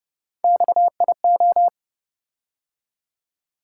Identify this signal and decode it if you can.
Morse code, heard as a keyed tone, about 31 wpm, 700 Hz. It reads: XIO